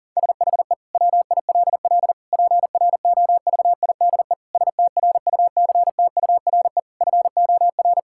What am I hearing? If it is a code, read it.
Morse code, heard as a keyed tone, about 40 wpm, 700 Hz. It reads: SHE WILL PROVIDE STRUCTURE FOR
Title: Sentences from Top 1000 Words - Review Part 6 - 40wpm